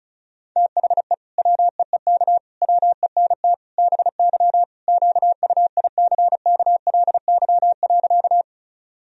Morse code, heard as a keyed tone, about 34 wpm, 700 Hz. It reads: THE WEEK WENT BY QUICKLY.
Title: Sentences from Top 1000 Words - Review Part 2 - 35wpm